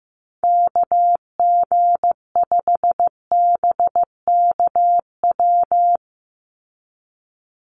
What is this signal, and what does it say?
Morse code, heard as a keyed tone, about 15 wpm, 700 Hz. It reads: KG5BKW
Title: USA Call Signs 15wpm